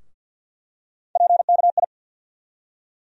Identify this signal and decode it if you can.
Morse code, heard as a keyed tone, about 52 wpm, 700 Hz. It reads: PKI
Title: USA Call Sign Postfixes - 3 Letters 50wpm